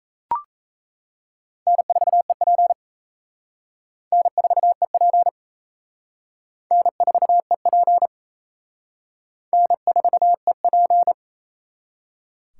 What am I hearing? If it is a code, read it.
Morse code, heard as a keyed tone, about 34 wpm, 700 Hz. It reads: N4EP N4EP N4EP N4EP
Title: Speed Racing - USA 1x2 Call Signs 28wpm